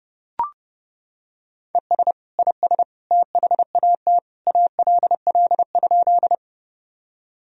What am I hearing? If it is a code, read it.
Morse code, heard as a keyed tone, about 30 wpm, 700 Hz. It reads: ES IS THAT ALL?
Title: Sentences from Top 100 Words and Abbreviations 30wpm